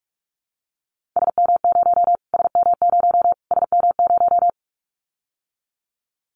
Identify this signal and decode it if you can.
Morse code, heard as a keyed tone, about 44 wpm, 700 Hz. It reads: SM0 SM0 SM0
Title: Callsign Prefixes for Top 25 DXCC - Repeated 3x - 45wpm